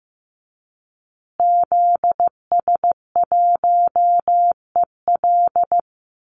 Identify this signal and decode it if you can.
Morse code, heard as a keyed tone, about 15 wpm, 700 Hz. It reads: ZS1EL
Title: Call Signs Repeated 2x - 15wpm